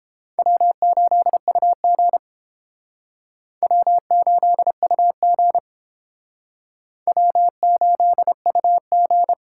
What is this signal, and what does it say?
Morse code, heard as a keyed tone, about 30 wpm, 700 Hz. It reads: W8UG W8UG W8UG